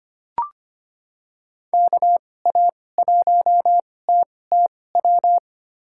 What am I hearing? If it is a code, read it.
Morse code, heard as a keyed tone, about 25 wpm, spaced as for 18 wpm, 700 Hz. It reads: KA1TTW
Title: USA Call Signs - 6 Characters - 2x Character Spacing - 25wpm